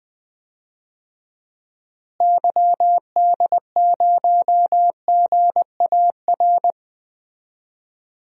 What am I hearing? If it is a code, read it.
Morse code, heard as a keyed tone, about 20 wpm, 700 Hz. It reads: YD0GAR
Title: DXCC-100 Call Signs - 6 Characters - Repeated 2x - 20wpm